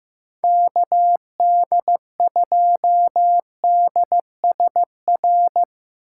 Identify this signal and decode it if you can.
Morse code, heard as a keyed tone, about 15 wpm, 700 Hz. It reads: KD2DSR
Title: USA Call Signs - 6x Character Spacing - 15wpm